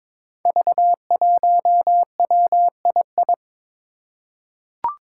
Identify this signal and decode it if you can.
Morse code, heard as a keyed tone, about 22 wpm, 700 Hz. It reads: V1WII
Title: Five Challenging Letter-Number Combinations 22wpm